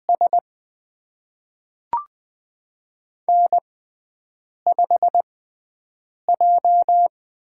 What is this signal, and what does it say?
Morse code, heard as a keyed tone, about 20 wpm, spaced as for 7 wpm, 700 Hz. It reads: S N5J
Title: USA Call Signs - 5 Characters - 6x Character Spacing - 20wpm